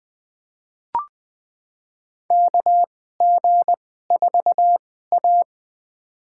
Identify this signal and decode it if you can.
Morse code, heard as a keyed tone, about 20 wpm, spaced as for 14 wpm, 700 Hz. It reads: KG4A
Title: USA Call Signs - 4 Characters - 2x Character Spacing - 20wpm